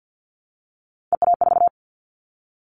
Morse code, heard as a keyed tone, about 50 wpm, 700 Hz. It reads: EA4
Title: Callsign Prefixes for Top 25 DXCC - Repeated 3x - 50wpm